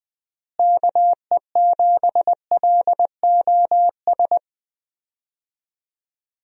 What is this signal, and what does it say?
Morse code, heard as a keyed tone, about 20 wpm, 700 Hz. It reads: KE7LOS
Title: USA Call Signs - 6 Characters - Repeated 3x - 20wpm